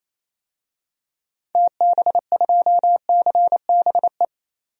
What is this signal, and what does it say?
Morse code, heard as a keyed tone, about 28 wpm, 700 Hz. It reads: TB2CBE
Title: DXCC-100 Call Signs - 6 Characters - 28wpm